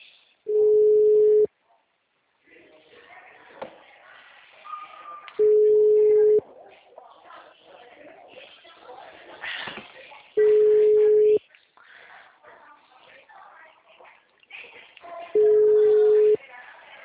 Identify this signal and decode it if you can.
Ringback tone (Europe, 425 Hz), with speech.